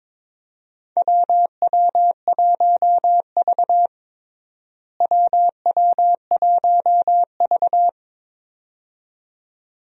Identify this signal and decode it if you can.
Morse code, heard as a keyed tone, about 22 wpm, 700 Hz. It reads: WW1V WW1V
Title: USA Call Signs - 4 Characters - Repeated 2x - 22wpm